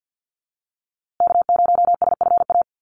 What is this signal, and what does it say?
Morse code, heard as a keyed tone, about 48 wpm, 700 Hz. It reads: K0SFA
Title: USA Call Signs Repeated 2x 50wpm